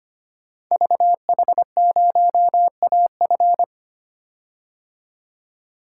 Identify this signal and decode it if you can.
Morse code, heard as a keyed tone, about 25 wpm, 700 Hz. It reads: VH0AF
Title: Code Groups - 5 Letter:Numbers - 25wpm